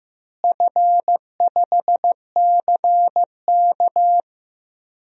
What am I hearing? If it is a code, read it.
Morse code, heard as a keyed tone, about 15 wpm, 700 Hz. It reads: F5CK
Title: DXCC-100 Call Signs - 4 Characters - Repeated 2x - 15wpm